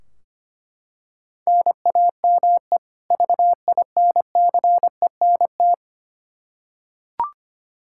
Morse code, heard as a keyed tone, about 25 wpm, 700 Hz. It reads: NAME VINCENT
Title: Practice QSO - Exchange 1 - Beginning - 25wpm